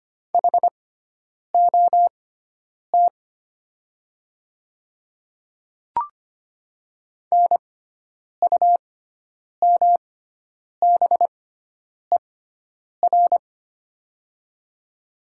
Morse code, heard as a keyed tone, about 25 wpm, spaced as for 9 wpm, 700 Hz. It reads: HOT NUMBER
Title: 6x Character Spacing - Top 100 Words - 25wpm